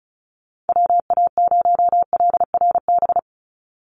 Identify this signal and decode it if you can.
Morse code, heard as a keyed tone, about 34 wpm, 700 Hz. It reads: WA0LRB
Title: USA Call Signs 35wpm